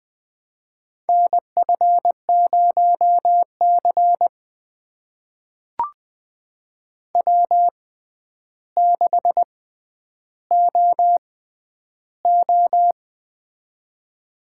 Morse code, heard as a keyed tone, about 20 wpm, spaced as for 10 wpm, 700 Hz. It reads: NF0C W6OO
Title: USA Call Signs - 4 Characters - 6x Character Spacing - 20wpm